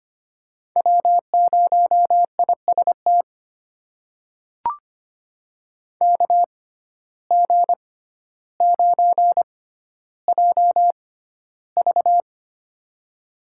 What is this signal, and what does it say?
Morse code, heard as a keyed tone, about 25 wpm, spaced as for 13 wpm, 700 Hz. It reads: W0IST KG9JV
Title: USA Call Signs - 5 Characters - 6x Character Spacing - 25wpm